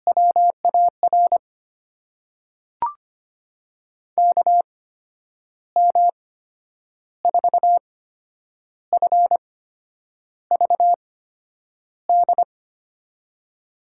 Morse code, heard as a keyed tone, about 25 wpm, spaced as for 9 wpm, 700 Hz. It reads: WAR KM4FVD
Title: USA Call Signs - 6 Characters - 8x Character Spacing - 25wpm